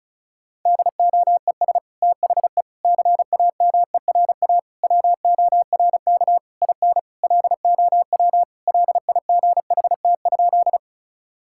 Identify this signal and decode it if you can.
Morse code, heard as a keyed tone, about 36 wpm, 700 Hz. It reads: DOES THE CAMERA WORK IN LOW LIGHT?
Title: Sentences from Top 1000 Words - Review Part 5 - 35wpm